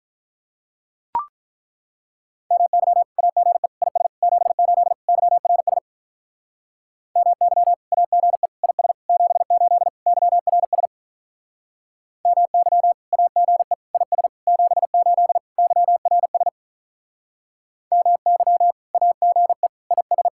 Morse code, heard as a keyed tone, about 43 wpm, 700 Hz. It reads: MY AGE IS 78 YRS MY AGE IS 78 YRS MY AGE IS 78 YRS MY AGE IS
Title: Speed-Racing Common QSO Phrases Encore - 35wpm